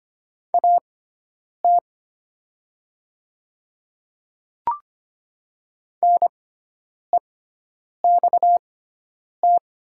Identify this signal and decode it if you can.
Morse code, heard as a keyed tone, about 25 wpm, spaced as for 9 wpm, 700 Hz. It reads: AT NEXT
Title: 6x Character Spacing - Top 500 Words - 25wpm